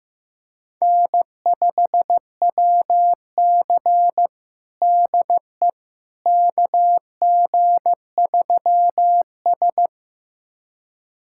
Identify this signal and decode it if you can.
Morse code, heard as a keyed tone, about 16 wpm, 700 Hz. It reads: N5WC DE KG3S
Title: USA Callsign DE Callsign - 4 Characters - 15wpm